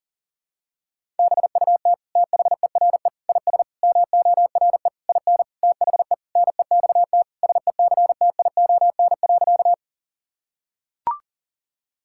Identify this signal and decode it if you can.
Morse code, heard as a keyed tone, about 40 wpm, 700 Hz. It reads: BUT THERE IS MORE IN THE NEXT SECTION.
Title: Sentences from Top 1000 Words - Review Part 5 - 40wpm